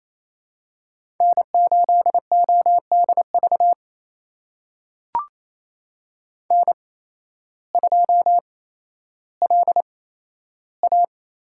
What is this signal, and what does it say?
Morse code, heard as a keyed tone, about 28 wpm, spaced as for 13 wpm, 700 Hz. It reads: N8ODV N2LA
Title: USA Call Signs - 8x Character Spacing - 28wpm